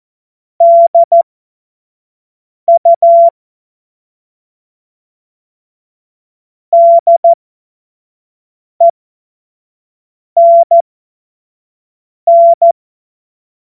Morse code, heard as a keyed tone, about 14 wpm, spaced as for 5 wpm, 680 Hz. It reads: DU DENN
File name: Morsekurs / Witze / Witz 22. Februar 2022 25 F70